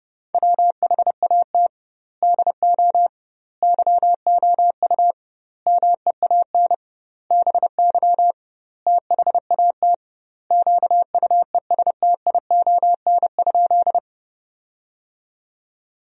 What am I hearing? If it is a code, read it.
Morse code, heard as a keyed tone, about 30 wpm, 700 Hz. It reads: WHAT DO YOU MEAN BY THAT QUESTION?
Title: 2x Word-Spacing - Sentences from Top 400 Words - 30wpm